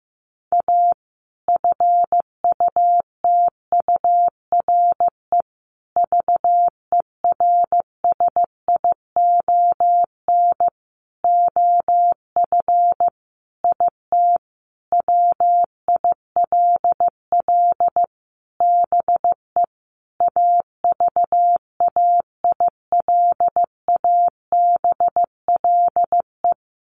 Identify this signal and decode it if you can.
Morse code, heard as a keyed tone, about 15 wpm, 700 Hz. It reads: A FUTURE VERSION OF IT WILL BE AVAILABLE
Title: Sentences from Top 1400 Words 15wpm